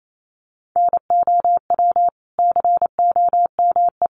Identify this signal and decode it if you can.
Morse code, heard as a keyed tone, about 28 wpm, 700 Hz. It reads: NOW COME
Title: Sentences from Top 200 Words 28wpm